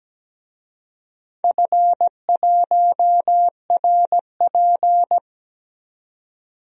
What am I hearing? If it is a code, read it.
Morse code, heard as a keyed tone, about 17 wpm, 700 Hz. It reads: F1RP
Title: DXCC-100 Call Signs - 4 Characters - 17wpm